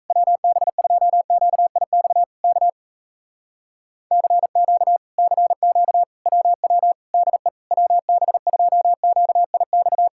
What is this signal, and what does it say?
Morse code, heard as a keyed tone, about 38 wpm, 700 Hz. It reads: WB2QIX K CQ CQ WW DE WB2QIX